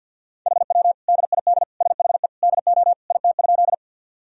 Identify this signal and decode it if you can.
Morse code, heard as a keyed tone, about 52 wpm, 700 Hz. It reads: HW DID SHE DO IT?
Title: Sentences from Top 100 Words and Abbreviations 50wpm